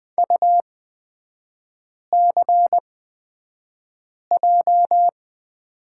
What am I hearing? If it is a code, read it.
Morse code, heard as a keyed tone, about 20 wpm, spaced as for 5 wpm, 700 Hz. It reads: UCJ
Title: 224 Lesson 36e - BT - Three Characters 20wpm